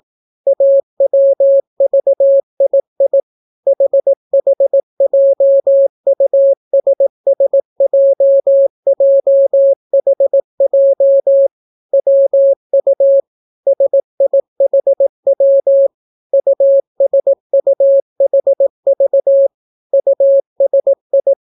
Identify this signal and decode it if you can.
Morse code, heard as a keyed tone, about 18 wpm, 550 Hz. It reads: AWVII HHJUSSJJHJ WU SIHW USUHV USI